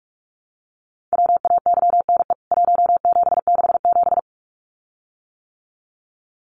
Morse code, heard as a keyed tone, about 44 wpm, 700 Hz. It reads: WAYNE 1767
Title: CWT Exchange 45wpm